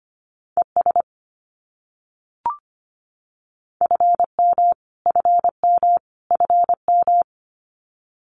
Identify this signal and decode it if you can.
Morse code, heard as a keyed tone, about 25 wpm, 700 Hz. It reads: ES FM FM FM
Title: Abbreviations Repeated 3x - 25wpm